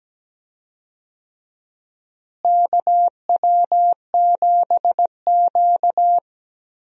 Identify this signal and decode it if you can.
Morse code, heard as a keyed tone, about 17 wpm, 700 Hz. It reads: KW7Q
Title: USA Call Signs - 4 Characters - Repeated 2x - 17wpm